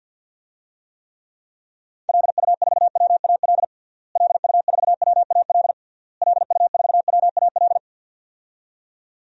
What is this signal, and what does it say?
Morse code, heard as a keyed tone, about 51 wpm, 700 Hz. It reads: LU4WAL LU4WAL LU4WAL
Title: DXCC-100 Call Signs - 6 Characters - Repeated 3x - 50wpm